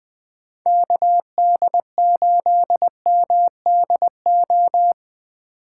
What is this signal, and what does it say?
Morse code, heard as a keyed tone, about 20 wpm, 700 Hz. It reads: KD8MDO
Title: USA Call Signs - 6 Characters - 4x Character Spacing - 20wpm